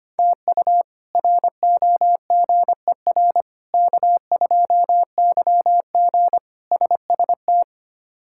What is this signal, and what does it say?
Morse code, heard as a keyed tone, about 25 wpm, 700 Hz. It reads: TU ROGER K2YG SST
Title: K1USN SST QSO Practice - 25wpm